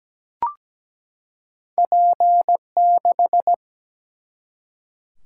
Morse code, heard as a keyed tone, about 17 wpm, 700 Hz. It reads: P6
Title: Single-Letter Single-Number 17wpm